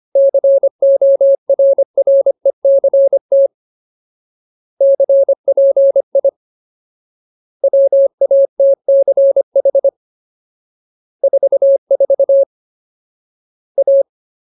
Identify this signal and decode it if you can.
Morse code, heard as a keyed tone, about 25 wpm, 550 Hz. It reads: CORRECT CPI WATCH 44 A
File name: Ultimate Word List / 25WPM Ultimate